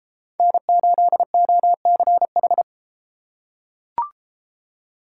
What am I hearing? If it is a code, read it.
Morse code, heard as a keyed tone, about 30 wpm, 700 Hz. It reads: N8OCH